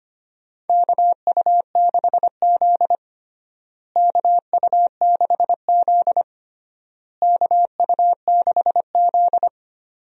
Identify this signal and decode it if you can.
Morse code, heard as a keyed tone, about 25 wpm, 700 Hz. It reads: KU6Z KU6Z KU6Z
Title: USA Call Signs - 4 Characters - Repeated 3x - 25wpm